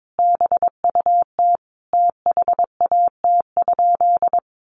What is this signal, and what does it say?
Morse code, heard as a keyed tone, about 22 wpm, 700 Hz. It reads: BUT THAT?
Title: Sentences from Top 1400 Words 22wpm